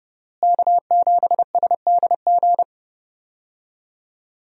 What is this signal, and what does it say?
Morse code, heard as a keyed tone, about 32 wpm, 700 Hz. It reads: K7SDG
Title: USA Call Signs - 5 Characters - Repeated 3x - 30wpm